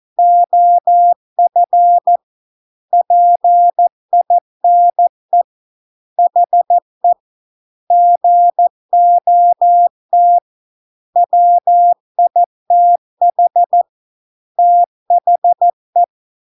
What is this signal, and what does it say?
Morse code, heard as a keyed tone, about 14 wpm, 700 Hz. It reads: OF PINE HE GOT WITH THE